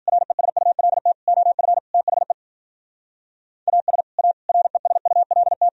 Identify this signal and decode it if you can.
Morse code, heard as a keyed tone, about 53 wpm, 700 Hz. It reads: RESULT OF THE AS A RESULT